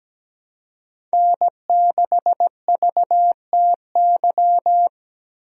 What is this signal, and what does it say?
Morse code, heard as a keyed tone, about 17 wpm, 700 Hz. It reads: N6VTY
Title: USA Call Signs - 5 Characters - Repeated 3x - 17wpm